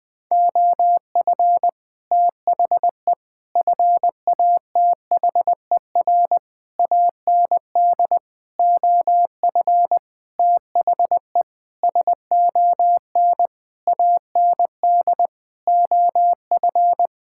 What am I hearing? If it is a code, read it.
Morse code, heard as a keyed tone, about 20 wpm, 700 Hz. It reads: OF THE FATHER AND OF THE SON AND OF
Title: Sets of 9 Words - 20wpm